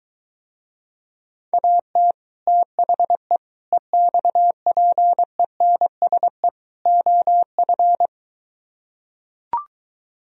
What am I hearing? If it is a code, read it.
Morse code, heard as a keyed tone, about 23 wpm, 700 Hz. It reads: AT THE EXPENSE OF